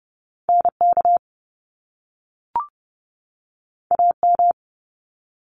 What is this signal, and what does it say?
Morse code, heard as a keyed tone, about 30 wpm, 700 Hz. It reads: NK AM
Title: Top Two Letter Combinations in Words 30wpm